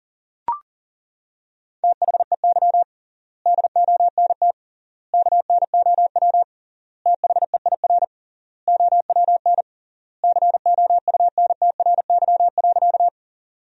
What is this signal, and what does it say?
Morse code, heard as a keyed tone, about 40 wpm, 700 Hz. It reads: THEY DONT KNOW THEIR OWN COUNTRY.
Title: 3x Word-Spacing - Sentences from Top 300 Words - 40wpm